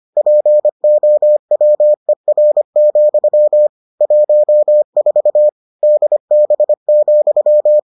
Morse code, heard as a keyed tone, about 25 wpm, 600 Hz. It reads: POWER, 14 DB,